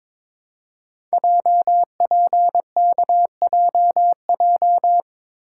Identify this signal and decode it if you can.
Morse code, heard as a keyed tone, about 22 wpm, 700 Hz. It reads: JPKJJ
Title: Code Groups - 5 Letter:Numbers - 22wpm